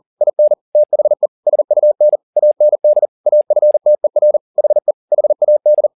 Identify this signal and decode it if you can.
Morse code, heard as a keyed tone, about 40 wpm, 600 Hz. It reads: IN THE SUN AND AFTER HE HAD